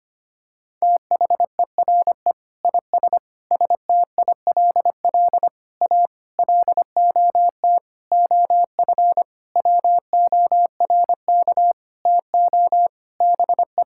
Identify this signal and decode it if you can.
Morse code, heard as a keyed tone, about 25 wpm, 700 Hz. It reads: THERE IS STILL A LOT OF WORK TO BE
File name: practice / 25wpm / Sentences with 9-Word Phrases - 25wpm